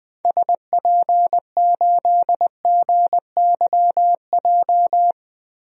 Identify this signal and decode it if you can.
Morse code, heard as a keyed tone, about 20 wpm, 700 Hz. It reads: SP8GYJ
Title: DXCC-100 Call Signs - 6 Characters - Repeated 2x - 20wpm